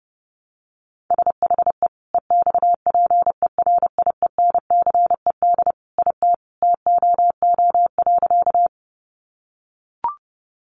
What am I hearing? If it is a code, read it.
Morse code, heard as a keyed tone, about 28 wpm, 700 Hz. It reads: SHE EXPERIENCED IT TOO.
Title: Sentences from Top 1100 Words 30wpm